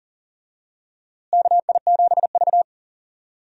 Morse code, heard as a keyed tone, about 40 wpm, 700 Hz. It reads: KI7V
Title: USA Call Signs - 4 Characters - Repeated 2x - 40wpm